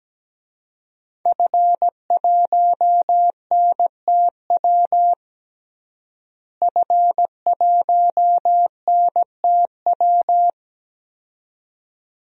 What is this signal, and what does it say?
Morse code, heard as a keyed tone, about 17 wpm, 700 Hz. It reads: F1NTW F1NTW
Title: DXCC-100 Call Signs - 5 Characters - Repeated 2x - 17wpm